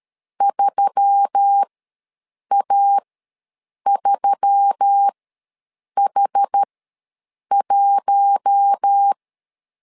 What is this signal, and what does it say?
Morse code, heard as a keyed tone, about 13 wpm, spaced as for 7 wpm, 800 Hz. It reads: 3A3H1
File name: LEZIONI / 4 Q1S3 ws 12wpm 2038